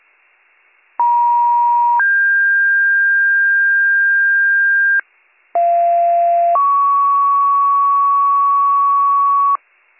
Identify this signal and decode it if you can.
Two-tone page, heard as a one-second tone, then a three-second tone, about 950 then 1600 Hz, about 690 then 1120 Hz, with noise. It is heard from 1.0 to 9.6 s.